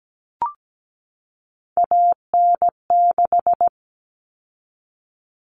Morse code, heard as a keyed tone, about 17 wpm, 700 Hz. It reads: AN6
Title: Common Cut Numbers - 3-Digit 17wpm